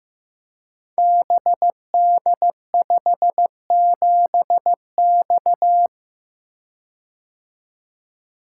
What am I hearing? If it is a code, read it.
Morse code, heard as a keyed tone, about 15 wpm, 700 Hz. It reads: BD57X
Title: Code Groups - 5 Letter:Numbers - 15wpm